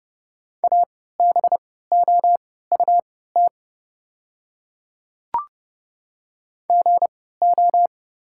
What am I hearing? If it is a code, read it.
Morse code, heard as a keyed tone, about 29 wpm, spaced as for 17 wpm, 690 Hz. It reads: ABOUT GO
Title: 3x Character Spacing - Top 200 Words - 30wpm